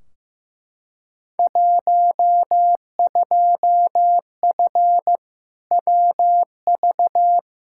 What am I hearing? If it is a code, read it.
Morse code, heard as a keyed tone, about 15 wpm, 700 Hz. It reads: 12F WV
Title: Field Day Exchanges Encore - 15wpm